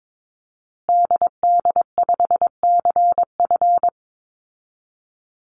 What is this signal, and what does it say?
Morse code, heard as a keyed tone, about 22 wpm, 700 Hz. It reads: DD5CF
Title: Call Signs Repeated 2x - 22wpm